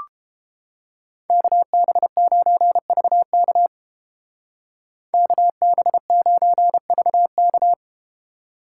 Morse code, heard as a keyed tone, about 33 wpm, 700 Hz. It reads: KB9VK KB9VK